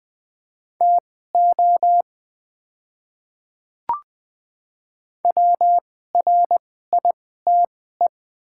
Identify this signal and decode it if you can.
Morse code, heard as a keyed tone, about 20 wpm, spaced as for 15 wpm, 700 Hz. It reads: TO WRITE
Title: 2x Character Spacing - Top 200 Words - 20wpm